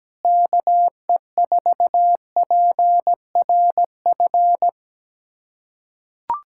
Morse code, heard as a keyed tone, about 17 wpm, 700 Hz. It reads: KE4PRF
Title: USA Callsign DE Callsign - 6 Characters - 17wpm